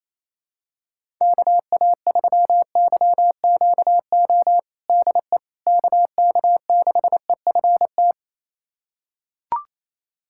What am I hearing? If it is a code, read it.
Morse code, heard as a keyed tone, about 28 wpm, 700 Hz. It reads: KA3YQO DE KK6EFT
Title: USA Callsign DE Callsign - 6 Characters - 28wpm